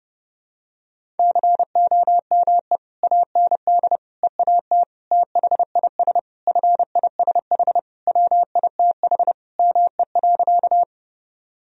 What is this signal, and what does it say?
Morse code, heard as a keyed tone, about 30 wpm, 700 Hz. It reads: COME AND EAT THIS FISH WITH ME.
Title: Sentences from Top 1000 Words - Review Part 1 - 30wpm